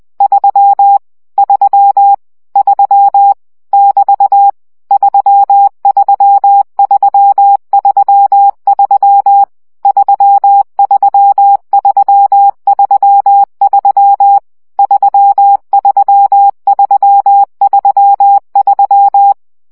Morse code, heard as a keyed tone, about 20 wpm, 800 Hz. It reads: 3 3 3 <BT> 33333 33333 33333
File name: CW gyakorlas 3 17 baud